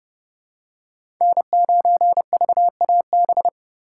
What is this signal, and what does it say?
Morse code, heard as a keyed tone, about 30 wpm, 700 Hz. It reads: N9VAB
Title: USA Call Signs - 5 Characters - 4x Character Spacing - 30wpm